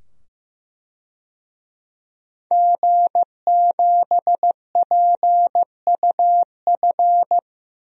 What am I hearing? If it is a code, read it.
Morse code, heard as a keyed tone, about 15 wpm, 700 Hz. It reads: G7PUF
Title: DXCC-100 Call Signs - 5 Characters - 15wpm